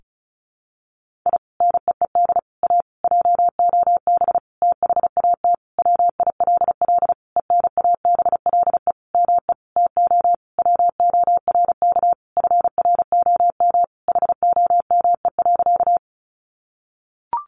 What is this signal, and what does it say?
Morse code, heard as a keyed tone, about 34 wpm, 700 Hz. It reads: I NEED A JOB THAT WILL ENABLE ME TO WORK FROM HOME.
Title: Sentences from Top 1500 Words 35wpm